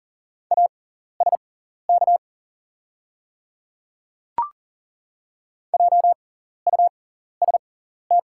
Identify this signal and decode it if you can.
Morse code, heard as a keyed tone, about 40 wpm, spaced as for 14 wpm, 700 Hz. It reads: ASK JUST
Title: 6x Character Spacing - Top 200 Words - 40wpm